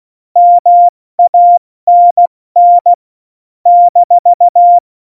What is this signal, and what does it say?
Morse code, heard as a keyed tone, about 16 wpm, spaced as for 14 wpm, 700 Hz. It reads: MANN -
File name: Morsekurs / Witze / Witz 06 March 2018, 70 F80